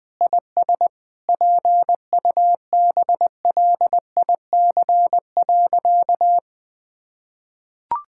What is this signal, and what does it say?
Morse code, heard as a keyed tone, about 20 wpm, 700 Hz. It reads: IS PUBLIC.
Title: Sentences from Top 1000 Words - Review Part 5 - 20wpm